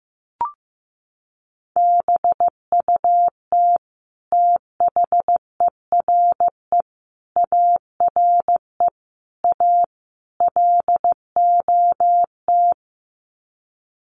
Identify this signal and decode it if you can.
Morse code, heard as a keyed tone, about 15 wpm, 700 Hz. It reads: BUT THERE ARE A LOT
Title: Sets of 5 Words Encore2 15wpm